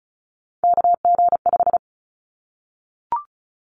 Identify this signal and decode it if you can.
Morse code, heard as a keyed tone, about 34 wpm, 700 Hz. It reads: KG5
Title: Callsign Prefixes for Top 25 DXCC - 35wpm